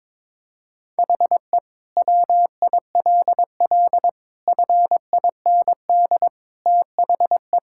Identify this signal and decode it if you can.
Morse code, heard as a keyed tone, about 22 wpm, 700 Hz. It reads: HE WILL FIND THE
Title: Sentences from Top 1000 Words - Review Part 6 - 22wpm